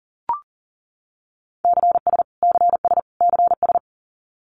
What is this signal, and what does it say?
Morse code, heard as a keyed tone, about 40 wpm, 700 Hz. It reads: CS CS CS
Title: Abbreviations Repeated 3x - 40wpm